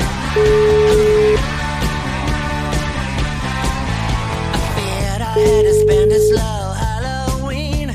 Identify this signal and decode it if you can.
Ringback tone (Europe, 425 Hz), with noise.